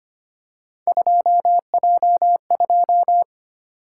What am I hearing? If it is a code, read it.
Morse code, heard as a keyed tone, about 25 wpm, 700 Hz. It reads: 2J2
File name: Three Letter-Number Combinations - 25wpm